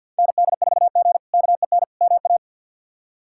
Morse code, heard as a keyed tone, about 52 wpm, 700 Hz. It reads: ND4G KEN MA
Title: K1USN SST QSO Practice - 50wpm